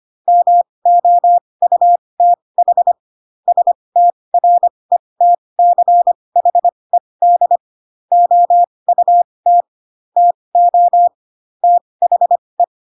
Morse code, heard as a keyed tone, about 25 wpm, spaced as for 20 wpm, 700 Hz. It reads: MOUTH STRETCHED OUT TO THE